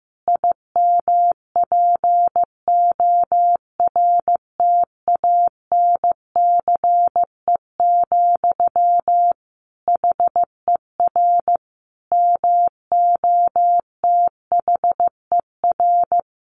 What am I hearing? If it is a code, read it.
Morse code, heard as a keyed tone, about 15 wpm, 700 Hz. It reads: IMPORTANCE, HER MOTHER
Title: Sentences from Top 1500 Words 15wpm